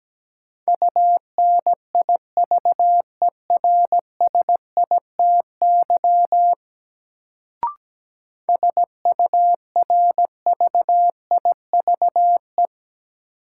Morse code, heard as a keyed tone, about 17 wpm, 700 Hz. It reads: UNIVERSITY SURVIVE
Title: Sentences from Top 1300 Words - New Word Drill Alt - 17wpm